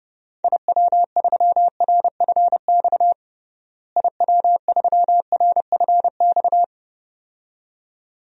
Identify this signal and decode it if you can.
Morse code, heard as a keyed tone, about 31 wpm, 700 Hz. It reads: IW3RFX IW3RFX
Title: DXCC-100 Call Signs - 6 Characters - Repeated 2x - 30wpm